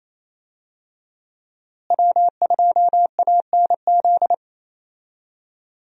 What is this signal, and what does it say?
Morse code, heard as a keyed tone, about 28 wpm, 700 Hz. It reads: W2ANZ
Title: USA Call Signs - 5 Characters - Repeated 3x - 28wpm